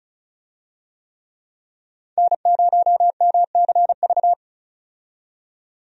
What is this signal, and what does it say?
Morse code, heard as a keyed tone, about 34 wpm, 700 Hz. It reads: N0MCV
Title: USA Call Signs - 5 Characters - 35wpm